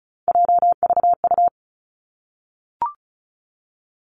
Morse code, heard as a keyed tone, about 36 wpm, 700 Hz. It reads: JVU
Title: Three Challenging Letter-Number Combinations 35wpm